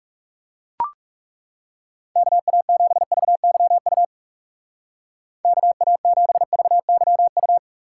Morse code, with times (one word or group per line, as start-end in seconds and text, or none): 2.2-4.0 KA7VYU
5.4-7.6 KA7VYU